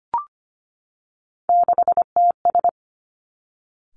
Morse code, heard as a keyed tone, about 25 wpm, 700 Hz. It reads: 6TS
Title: Three Challenging Letter-Number Combinations 25wpm